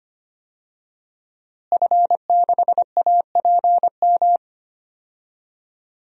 Morse code, heard as a keyed tone, about 25 wpm, 700 Hz. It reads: F6APM
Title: DXCC-100 Call Signs - 5 Characters - Repeated 3x - 25wpm